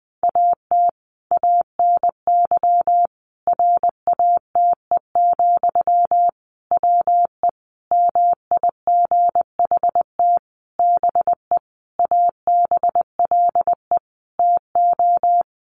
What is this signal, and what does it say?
Morse code, heard as a keyed tone, about 19 wpm, 700 Hz. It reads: AT ANY RATE, WE MIGHT BE ABLE TO
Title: Sentences from Top 700 Words 20wpm